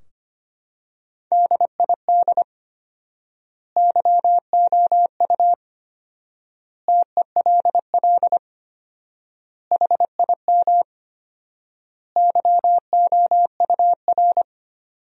Morse code, heard as a keyed tone, about 25 wpm, 700 Hz. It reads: DID YOU TELL HIM YOUR
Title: 4x Word-Spacing - Sentences from Top 400 Words - 25wpm